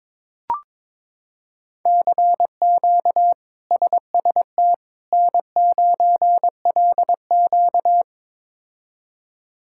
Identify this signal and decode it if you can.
Morse code, heard as a keyed tone, about 22 wpm, 700 Hz. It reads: CQ SST N9LQ
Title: K1USN SST QSO Practice - 22wpm